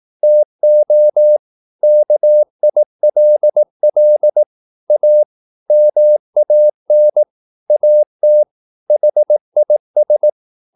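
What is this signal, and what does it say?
Morse code, heard as a keyed tone, about 18 wpm, 600 Hz. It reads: TO KILL A MAN AT HIS
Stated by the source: Morse practice at 18 wpm, English text, from CW med Gnister.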